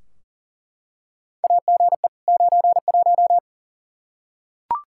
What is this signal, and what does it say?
Morse code, heard as a keyed tone, about 40 wpm, 700 Hz. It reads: AGE 91
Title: Practice QSO - Exchange 3 - More Info - 40wpm